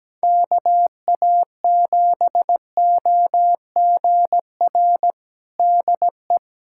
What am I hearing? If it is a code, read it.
Morse code, heard as a keyed tone, about 17 wpm, 700 Hz. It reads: KA7OGR DE
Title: USA Callsign DE Callsign - 6 Characters - 17wpm